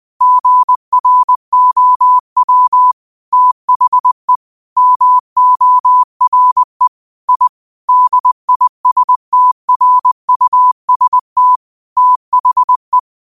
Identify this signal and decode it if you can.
Morse code, heard as a keyed tone, about 20 wpm, 1000 Hz. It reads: GROW THE MORE I DISTRUST THE